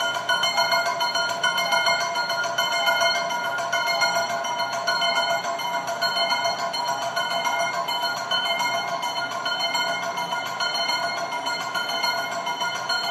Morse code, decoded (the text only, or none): none